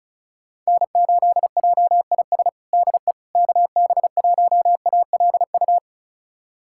Morse code, heard as a keyed tone, about 36 wpm, 700 Hz. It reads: N8JIS DE KB1ALU
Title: USA Callsign DE Callsign 35wpm